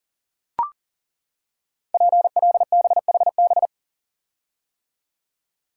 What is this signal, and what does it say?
Morse code, heard as a keyed tone, about 42 wpm, 700 Hz. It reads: PLBHB
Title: Code Groups - 5 Letter:Numbers - 40wpm